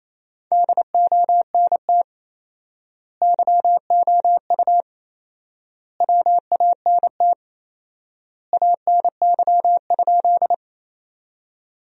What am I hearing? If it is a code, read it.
Morse code, heard as a keyed tone, about 28 wpm, 700 Hz. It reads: DONT YOU WANT ANY?
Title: 4x Word-Spacing - Sentences from Top 300 Words - 28wpm